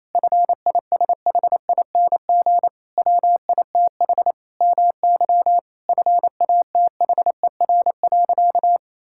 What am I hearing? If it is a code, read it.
Morse code, heard as a keyed tone, about 28 wpm, 700 Hz. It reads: FISHING WITH MY FATHER.
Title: Sentences from Top 1000 Words - Review Part 5 - 28wpm